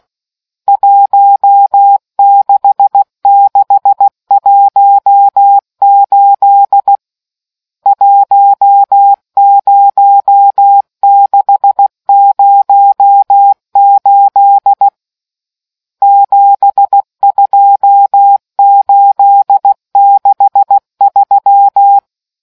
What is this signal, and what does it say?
Morse code, heard as a keyed tone, about 16 wpm, 800 Hz. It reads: 16618 10608 72863